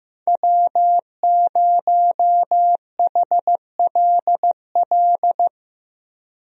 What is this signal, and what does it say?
Morse code, heard as a keyed tone, about 15 wpm, 700 Hz. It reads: W0HLL
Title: USA Call Signs - 5 Characters - Repeated 2x - 15wpm